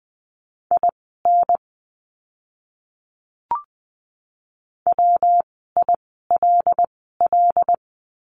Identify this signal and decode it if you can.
Morse code, heard as a keyed tone, about 20 wpm, spaced as for 14 wpm, 700 Hz. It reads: IN WILL
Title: 192 Lesson 31f - Letter v - Words 20wpm